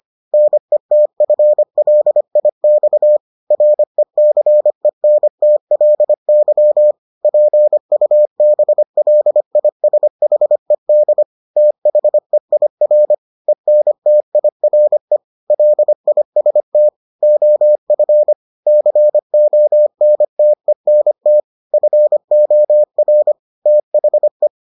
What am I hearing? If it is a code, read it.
Morse code, heard as a keyed tone, about 25 wpm, 600 Hz. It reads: NETFLIX RECENTLY PUBLISHED THEIR ENTIRE LIST OF CONTENT FOR THE